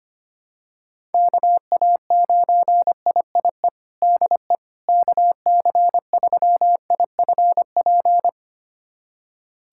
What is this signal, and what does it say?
Morse code, heard as a keyed tone, about 25 wpm, 700 Hz. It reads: KA9IIE DE KC3IFP
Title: USA Callsign DE Callsign 25wpm